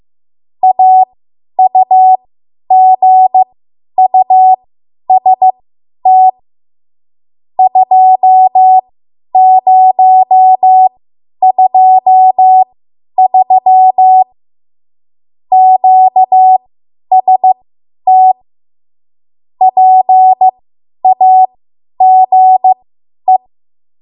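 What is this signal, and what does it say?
Morse code, heard as a keyed tone, about 15 wpm, spaced as for 10 wpm, 750 Hz. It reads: AUGUST 2023 QST PAGE
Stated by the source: Morse code practice run at 10 WPM from a code practice archive.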